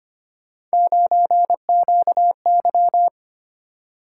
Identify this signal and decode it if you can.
Morse code, heard as a keyed tone, about 25 wpm, 700 Hz. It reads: 9QY
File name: Three Letter-Number Combinations - 25wpm